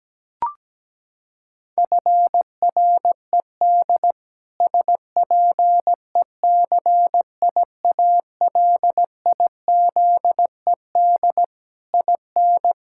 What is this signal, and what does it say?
Morse code, heard as a keyed tone, about 17 wpm, 700 Hz. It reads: FRED SPECIALIZED IN
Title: Pangram Challenge v2 - 17wpm